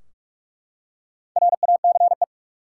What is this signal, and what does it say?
Morse code, heard as a keyed tone, about 44 wpm, 700 Hz. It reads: RACE
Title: Four Letter Words - 45wpm